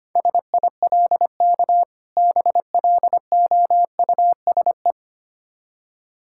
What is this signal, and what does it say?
Morse code, heard as a keyed tone, about 25 wpm, 700 Hz. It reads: SILK BLOUSE
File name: practice / 25wpm / Difficult Words with Hints - 25wpm